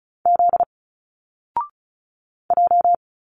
Morse code, heard as a keyed tone, about 35 wpm, 700 Hz. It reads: Z J
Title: Single Letters 35wpm